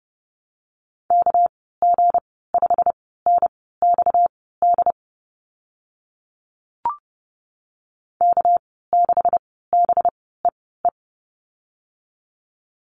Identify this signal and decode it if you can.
Morse code, heard as a keyed tone, about 30 wpm, spaced as for 17 wpm, 700 Hz. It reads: KG5NXD K6BEE
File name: Lesson 37g - Letter x - Call Signs 30wpm